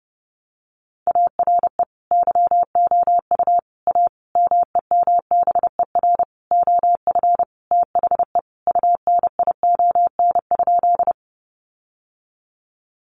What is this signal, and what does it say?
Morse code, heard as a keyed tone, about 30 wpm, 700 Hz. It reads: ARE YOU A MEMBER OF THE UNION?
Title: Sentences from Top 1400 Words 30wpm